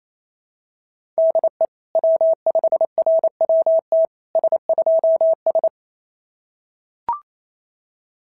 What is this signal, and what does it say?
Morse code, heard as a keyed tone, about 28 wpm, 690 Hz. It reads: DE W5RWT S2S
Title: SOTA QSO Practice 28wpm